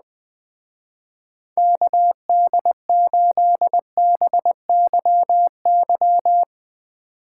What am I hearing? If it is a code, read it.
Morse code, heard as a keyed tone, about 20 wpm, 700 Hz. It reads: KD8BYY